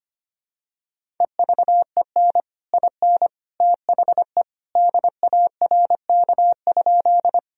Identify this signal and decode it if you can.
Morse code, heard as a keyed tone, about 25 wpm, 700 Hz. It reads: EVEN IN THE DARK?
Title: Sentences from Top 1000 Words - Review Part 1 - 25wpm